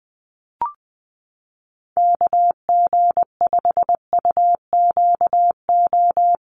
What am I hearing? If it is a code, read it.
Morse code, heard as a keyed tone, about 20 wpm, 700 Hz. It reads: KG5UQO
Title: USA Call Signs 20wpm